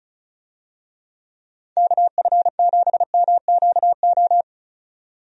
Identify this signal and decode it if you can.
Morse code, heard as a keyed tone, about 36 wpm, 700 Hz. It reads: KF7MQO
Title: USA Call Signs - 6 Characters - 35wpm